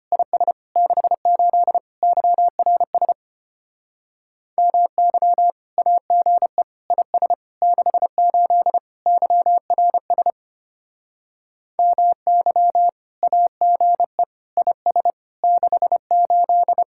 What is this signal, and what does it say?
Morse code, heard as a keyed tone, about 31 wpm, 700 Hz. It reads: IS 68 YRS MY AGE IS 68 YRS MY AGE IS 68
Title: Speed-Racing Common QSO Phrases - 25wpm